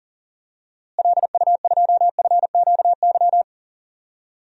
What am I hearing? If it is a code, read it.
Morse code, heard as a keyed tone, about 37 wpm, 700 Hz. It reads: LU2FQY
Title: DXCC-100 Call Signs - 6 Characters - Repeated 2x - 40wpm